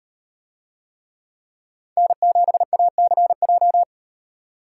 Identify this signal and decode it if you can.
Morse code, heard as a keyed tone, about 38 wpm, 700 Hz. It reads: N7ACJ